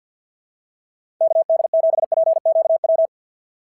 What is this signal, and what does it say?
Morse code, heard as a keyed tone, about 46 wpm, 650 Hz. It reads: KD7PQW
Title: SOTA QSO Practice 50wpm